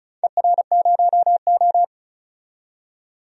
Morse code, heard as a keyed tone, about 36 wpm, 700 Hz. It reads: ER0O
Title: DXCC-100 Call Signs - 4 Characters - 35wpm